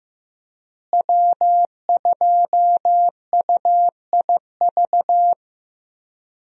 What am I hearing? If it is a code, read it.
Morse code, heard as a keyed tone, about 15 wpm, 700 Hz. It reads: W2UIV
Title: USA Call Signs - 5 Characters - Repeated 3x - 15wpm